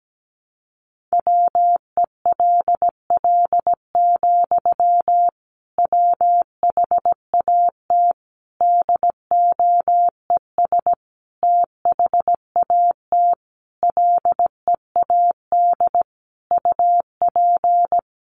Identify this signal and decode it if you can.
Morse code, heard as a keyed tone, about 17 wpm, 700 Hz. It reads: WELL, WHAT DOES THAT LEAD UP
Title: Sentences from Top 500 Words 17wpm